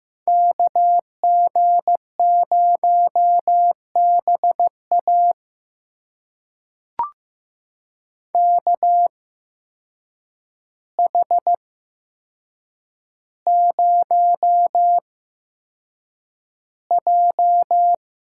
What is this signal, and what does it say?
Morse code, heard as a keyed tone, about 15 wpm, spaced as for 7 wpm, 700 Hz. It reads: KG0BA KH0J
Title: USA Call Signs - 5 Characters - 8x Character Spacing - 15wpm